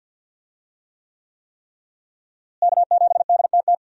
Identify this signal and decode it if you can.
Morse code, heard as a keyed tone, about 51 wpm, 700 Hz. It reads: K7DTT
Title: USA Call Signs - 5 Characters - Repeated 3x - 50wpm